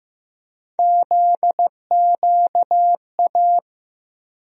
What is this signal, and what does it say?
Morse code, heard as a keyed tone, about 15 wpm, 700 Hz. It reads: ZQA
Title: USA Call Sign Postfixes - 3 Letters 15wpm